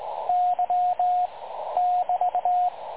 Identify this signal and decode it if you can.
Morse code, heard as a keyed tone, about 20 wpm, 720 Hz, with radio noise. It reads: Y <BT>
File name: dlazaawansowanych / nagraniazpasma / websdr recording 3565.0kHz